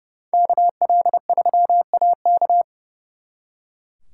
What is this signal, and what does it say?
Morse code, heard as a keyed tone, about 38 wpm, 700 Hz. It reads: KL3AK